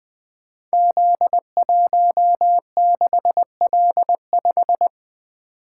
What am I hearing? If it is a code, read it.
Morse code, heard as a keyed tone, about 20 wpm, 700 Hz. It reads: Z16L5
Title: Code Groups - 5 Letter:Numbers - 20wpm